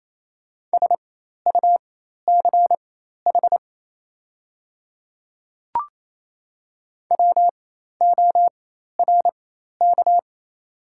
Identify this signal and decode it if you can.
Morse code, heard as a keyed tone, about 28 wpm, spaced as for 13 wpm, 700 Hz. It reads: SUCH WORK
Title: 4x Character Spacing - Top 200 Words - 28wpm